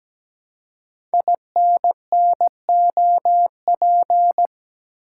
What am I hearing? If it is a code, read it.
Morse code, heard as a keyed tone, about 17 wpm, 700 Hz. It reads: INNOP
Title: Code Groups - 5 Letters - 17wpm